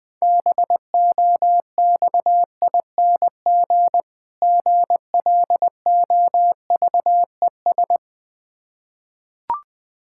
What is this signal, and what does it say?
Morse code, heard as a keyed tone, about 20 wpm, 690 Hz. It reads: BOXING GLOVES
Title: Difficult Words with Hints - 20wpm